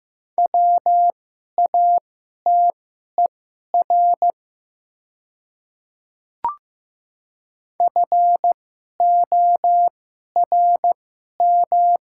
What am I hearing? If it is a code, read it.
Morse code, heard as a keyed tone, about 15 wpm, spaced as for 11 wpm, 700 Hz. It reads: WATER FORM
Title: 2x Character Spacing - Top 300 Words - 15wpm